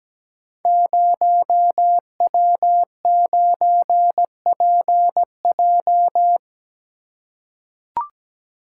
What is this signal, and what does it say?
Morse code, heard as a keyed tone, about 17 wpm, 700 Hz. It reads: 0W9PJ
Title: Code Groups - 5 Letter:Numbers - 17wpm